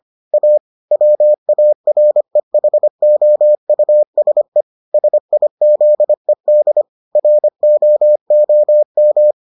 Morse code, heard as a keyed tone, about 25 wpm, 600 Hz. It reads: A WAREHOUSE SIZED ROOM